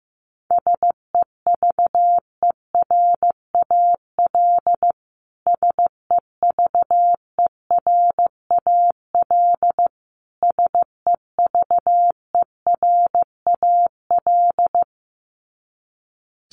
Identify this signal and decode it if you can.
Morse code, heard as a keyed tone, about 14 wpm, 700 Hz. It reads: SEVERAL SEVERAL SEVERAL
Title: Top 500 Words Repeated 15wpm